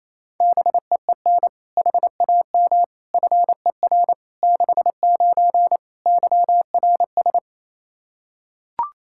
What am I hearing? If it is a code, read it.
Morse code, heard as a keyed tone, about 28 wpm, 700 Hz. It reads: BEEN HAM FER 69 YRS
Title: Practice QSO - Exchange 3 - More Info - 28wpm